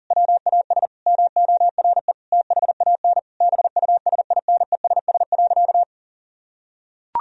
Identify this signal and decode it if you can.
Morse code, heard as a keyed tone, about 40 wpm, 700 Hz. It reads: WAS MORE THAN BUSINESS.
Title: Sentences from Top 1000 Words - Review Part 2 - 40wpm